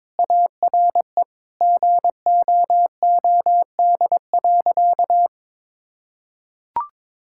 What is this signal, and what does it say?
Morse code, heard as a keyed tone, about 22 wpm, 700 Hz. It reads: ARE GOOD.
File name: practice / 22wpm / Sentences with 3-Word Phrases - 22wpm